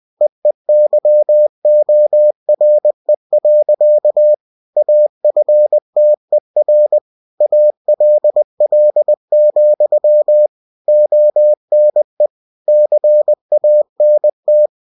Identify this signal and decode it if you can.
Morse code, heard as a keyed tone, about 20 wpm, 600 Hz. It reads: EEYORE. AFTER ALL, ONE CANT
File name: cw-books / winnie-the-pooh / winnie-20wpm-page-0108